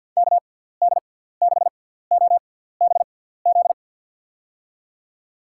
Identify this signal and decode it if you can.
Morse code, heard as a keyed tone, about 53 wpm, spaced as for 17 wpm, 700 Hz. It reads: KD6OBZ
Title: USA Call Signs - 6x Character Spacing - 50wpm